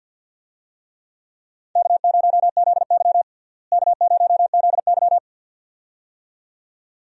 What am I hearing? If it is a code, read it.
Morse code, heard as a keyed tone, about 52 wpm, 700 Hz. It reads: K0ZY K0ZY
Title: USA Call Signs - 4 Characters - Repeated 2x - 50wpm